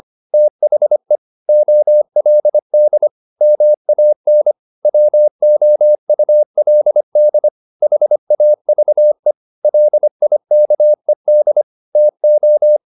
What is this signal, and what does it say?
Morse code, heard as a keyed tone, about 25 wpm, 600 Hz. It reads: THE OLD MAN WOULD HAVE LIKED TO
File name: the old man and the sea part 1 25wpm